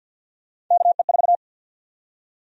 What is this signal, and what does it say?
Morse code, heard as a keyed tone, about 49 wpm, 700 Hz. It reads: KE4
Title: USA Call Sign Prefixes - 50wpm